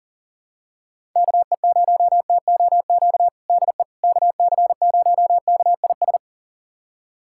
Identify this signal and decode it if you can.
Morse code, heard as a keyed tone, about 40 wpm, 700 Hz. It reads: KE0TOQ DE KC0KIS
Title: USA Callsign DE Callsign - 6 Characters - 40wpm